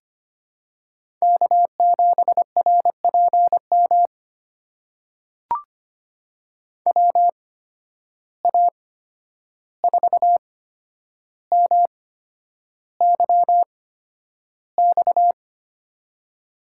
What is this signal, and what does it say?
Morse code, heard as a keyed tone, about 25 wpm, spaced as for 10 wpm, 700 Hz. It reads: K7RPM WA4MYX
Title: USA Call Signs - 8x Character Spacing - 25wpm